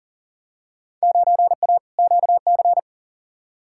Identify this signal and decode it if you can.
Morse code, heard as a keyed tone, about 40 wpm, 700 Hz. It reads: 9A QC
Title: Field Day Exchanges Encore - 40wpm